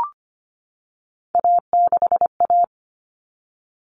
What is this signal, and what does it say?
Morse code, heard as a keyed tone, about 25 wpm, 700 Hz. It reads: A6A
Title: Common Cut Numbers - 3-Digit 25wpm